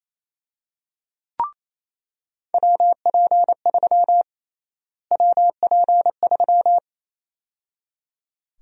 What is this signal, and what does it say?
Morse code, heard as a keyed tone, about 28 wpm, 700 Hz. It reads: WP3 WP3
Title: USA Call Sign Prefixes - Repeated 2x - 28wpm